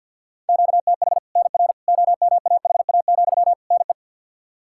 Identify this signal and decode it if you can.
Morse code, heard as a keyed tone, about 48 wpm, 700 Hz. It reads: QTH NR OMAHA, NE
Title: Practice QSO - Exchange 1 - Beginning - 50wpm